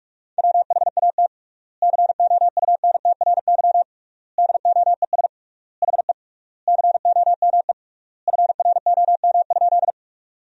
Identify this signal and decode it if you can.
Morse code, heard as a keyed tone, about 46 wpm, 700 Hz. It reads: WHAT COUNTRY DOES HE COME FROM?
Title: 3x Word-Spacing - Sentences from Top 300 Words - 45wpm